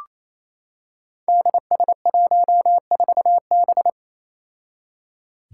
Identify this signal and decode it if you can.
Morse code, heard as a keyed tone, about 28 wpm, 700 Hz. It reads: DS14B
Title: Five Challenging Letter-Number Combinations 28wpm